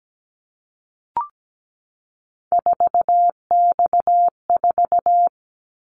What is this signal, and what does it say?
Morse code, heard as a keyed tone, about 17 wpm, 700 Hz. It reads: 4X4
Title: Callsign Prefixes for Top 50 DXCC - 17wpm